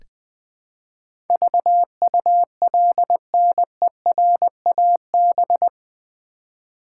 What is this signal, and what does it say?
Morse code, heard as a keyed tone, about 20 wpm, 700 Hz. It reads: VULNERAB
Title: Build A Word - 4 Letters At A Time - 10-Letter Words - 20wpm